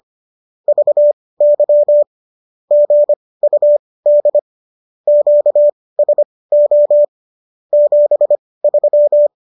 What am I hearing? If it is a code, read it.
Morse code, heard as a keyed tone, about 25 wpm, spaced as for 18 wpm, 590 Hz. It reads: VY GUD QSO 73